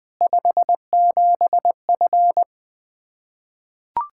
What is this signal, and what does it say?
Morse code, heard as a keyed tone, about 20 wpm, 700 Hz. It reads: 57F
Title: Practice QSO - Exchange 2 - Info - 20wpm